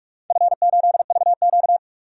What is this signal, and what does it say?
Morse code, heard as a keyed tone, about 44 wpm, 690 Hz. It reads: F8VQ
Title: DXCC-100 Call Signs - 4 Characters - 45wpm